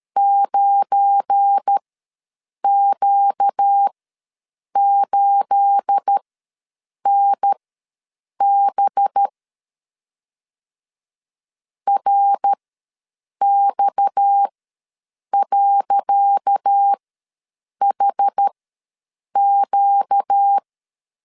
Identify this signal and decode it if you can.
Morse code, heard as a keyed tone, about 13 wpm, spaced as for 7 wpm, 800 Hz. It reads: 9Q8NB RX.HQ
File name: LEZIONI / 10 SIGNS / 10C DOT ws 12wpm 3548